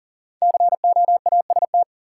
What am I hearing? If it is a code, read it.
Morse code, heard as a keyed tone, about 40 wpm, 700 Hz. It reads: COAST
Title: Sentences from Top 1300 Words - New Word Drill Alt - 40wpm